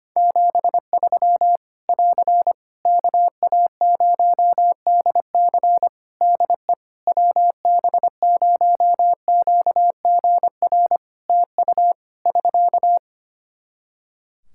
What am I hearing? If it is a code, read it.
Morse code, heard as a keyed tone, about 25 wpm, 700 Hz. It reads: 73 <AR> KA0DC DE WB0QGR TU <SK>
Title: Practice QSO - Exchange 4 - Ending - 25wpm